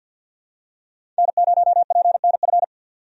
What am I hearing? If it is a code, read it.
Morse code, heard as a keyed tone, about 47 wpm, 700 Hz. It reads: N0PNF
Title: USA Call Signs - 5 Characters - 50wpm